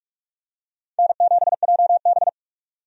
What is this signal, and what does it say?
Morse code, heard as a keyed tone, about 47 wpm, 700 Hz. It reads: N7JB
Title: USA Call Signs - 4 Characters - 45wpm